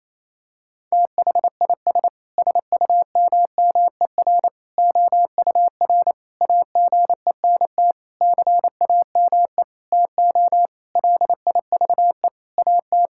Morse code, heard as a keyed tone, about 28 wpm, 700 Hz. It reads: THIS SUMMER OUR AGENT CAME TO LIVE AT
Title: Sentences from Top 1600 Words 28wpm